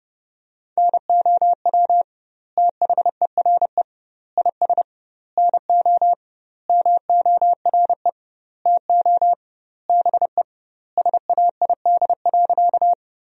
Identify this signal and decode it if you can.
Morse code, heard as a keyed tone, about 32 wpm, 700 Hz. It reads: NOW THERE IS NO MORE TO BE SAID.
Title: 2x Word-Spacing - Sentences from Top 100 Words - 30wpm